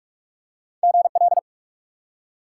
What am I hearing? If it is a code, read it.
Morse code, heard as a keyed tone, about 46 wpm, 700 Hz. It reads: GL
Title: Top 90 QSO Elements 45wpm